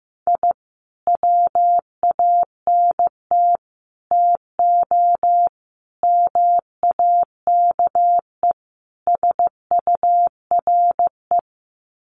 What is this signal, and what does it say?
Morse code, heard as a keyed tone, about 15 wpm, 700 Hz. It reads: I WANT TO MAKE SURE
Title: Sets of 5 Words Encore2 15wpm